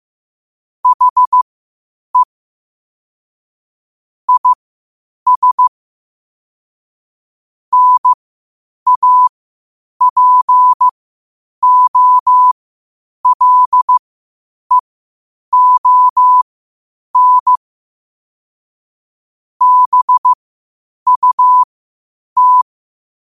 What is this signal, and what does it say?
Morse code, heard as a keyed tone, about 15 wpm, spaced as for 8 wpm, 1000 Hz. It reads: HE IS NAPOLEON BUT
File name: qotdfiles / 8wpm